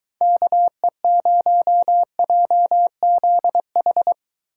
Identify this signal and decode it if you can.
Morse code, heard as a keyed tone, about 23 wpm, 700 Hz. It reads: KE0JZH